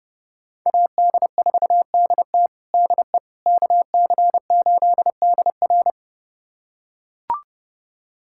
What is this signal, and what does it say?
Morse code, heard as a keyed tone, about 30 wpm, 700 Hz. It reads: AD4DT DE KC8DR
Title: USA Callsign DE Callsign - 5 Characters - 30wpm